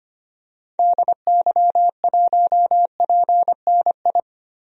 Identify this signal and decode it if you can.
Morse code, heard as a keyed tone, about 25 wpm, 700 Hz. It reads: DY1PNI
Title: DXCC-100 Call Signs - 6 Characters - Repeated 2x - 25wpm